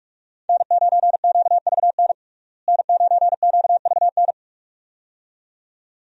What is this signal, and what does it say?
Morse code, heard as a keyed tone, about 43 wpm, 700 Hz. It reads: N9QVN N9QVN
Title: USA Call Signs - 5 Characters - Repeated 2x - 45wpm